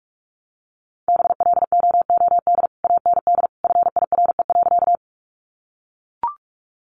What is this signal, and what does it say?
Morse code, heard as a keyed tone, about 44 wpm, 700 Hz. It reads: BLOOD AND FIRE.
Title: Sentences from Top 1100 Words 45wpm